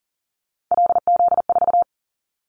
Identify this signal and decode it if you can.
Morse code, heard as a keyed tone, about 40 wpm, 700 Hz. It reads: LZ4
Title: Callsign Prefixes for Top 50 DXCC - 40wpm